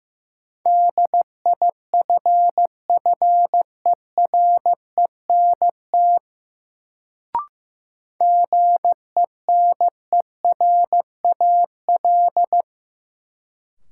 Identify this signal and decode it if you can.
Morse code, heard as a keyed tone, about 15 wpm, 700 Hz. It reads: DIFFERENT GENERAL
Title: Sentences from Top 600 Words - New Word Drill - 15wpm